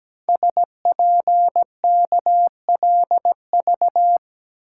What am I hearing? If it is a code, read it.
Morse code, heard as a keyed tone, about 17 wpm, 700 Hz. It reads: SPKLV
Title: Code Groups - 5 Letters - 17wpm